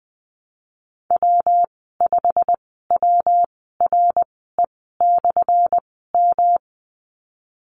Morse code, heard as a keyed tone, about 19 wpm, spaced as for 14 wpm, 700 Hz. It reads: W5WRE/M
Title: 179 Lesson 29g - Slash - Call Signs 20wpm